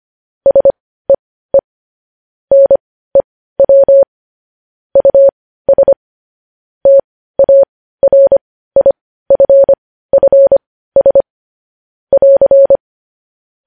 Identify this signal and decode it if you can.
Morse code, heard as a keyed tone, about 25 wpm, spaced as for 15 wpm, 550 Hz. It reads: SEE NEW US TARIFFS <AR>
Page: news headlines in morse code at 15 wpm